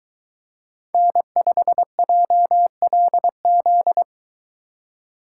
Morse code, heard as a keyed tone, about 23 wpm, 700 Hz. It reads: N5JLZ